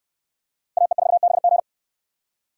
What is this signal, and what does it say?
Morse code, heard as a keyed tone, about 69 wpm, 700 Hz. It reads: AE4BG